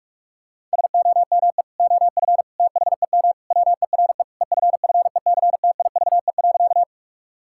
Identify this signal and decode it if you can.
Morse code, heard as a keyed tone, about 46 wpm, 700 Hz. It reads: SOME OF THEM WERE EFFECTIVE.
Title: Sentences from Top 1000 Words - Review Part 6 - 45wpm